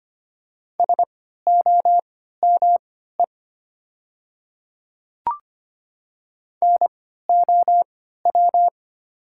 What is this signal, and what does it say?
Morse code, heard as a keyed tone, about 25 wpm, spaced as for 14 wpm, 700 Hz. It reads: SOME NOW
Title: 3x Character Spacing - Top 100 Words - 25wpm